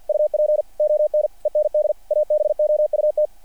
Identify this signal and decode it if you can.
Morse code, heard as a keyed tone, about 48 wpm, 600 Hz. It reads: UP ON END ABOUT
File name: archived / CRSnet